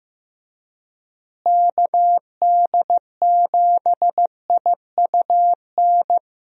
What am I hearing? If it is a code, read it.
Morse code, heard as a keyed tone, about 16 wpm, 700 Hz. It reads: KD7IUN
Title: USA Call Signs - 6 Characters - Repeated 3x - 15wpm